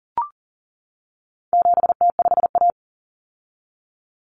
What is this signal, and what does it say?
Morse code, heard as a keyed tone, about 40 wpm, 700 Hz. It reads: 7T5A
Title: Common Cut Numbers - 4-Digit 40wpm